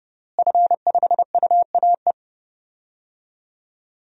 Morse code, heard as a keyed tone, about 31 wpm, 700 Hz. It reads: F5UAE
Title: DXCC-100 Call Signs - 5 Characters - Repeated 3x - 30wpm